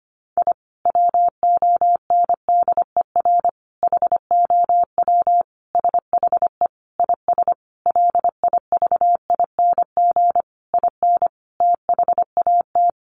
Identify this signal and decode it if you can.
Morse code, heard as a keyed tone, about 25 wpm, 700 Hz. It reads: I WONDER HOW SHE IS LIVING IN THAT
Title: Sentences from Top 1500 Words 25wpm